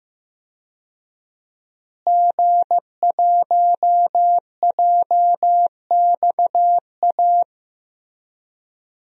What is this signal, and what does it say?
Morse code, heard as a keyed tone, about 15 wpm, 700 Hz. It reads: G1JXA
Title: DXCC-100 Call Signs - 5 Characters - Repeated 2x - 15wpm